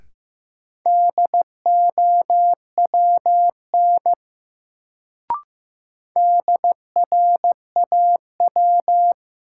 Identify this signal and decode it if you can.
Morse code, heard as a keyed tone, about 16 wpm, 700 Hz. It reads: DOWN DRAW
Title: Sentences from Top 300 Words - New Word Drill - 15wpm